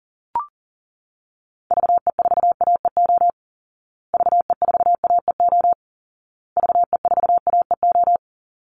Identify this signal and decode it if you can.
Morse code, heard as a keyed tone, about 40 wpm, 700 Hz. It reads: VE4AEO VE4AEO VE4AEO
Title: Call Signs Repeated 2x - 40wpm